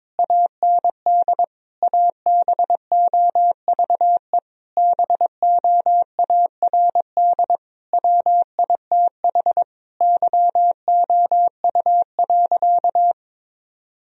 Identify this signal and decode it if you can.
Morse code, heard as a keyed tone, about 22 wpm, 700 Hz. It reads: AND ABOVE BOARD WITH YOU.
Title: Sentences from Top 1000 Words - Review Part 4 - 22wpm